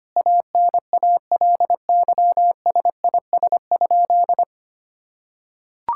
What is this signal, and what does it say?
Morse code, heard as a keyed tone, about 25 wpm, 700 Hz. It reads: ANALYSIS?
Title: Sentences from Top 1000 Words - Review Part 5 - 25wpm